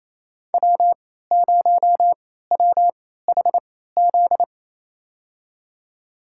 Morse code, heard as a keyed tone, about 28 wpm, spaced as for 16 wpm, 690 Hz. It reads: W0WHZ
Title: USA Call Signs - 3x Character Spacing - 28wpm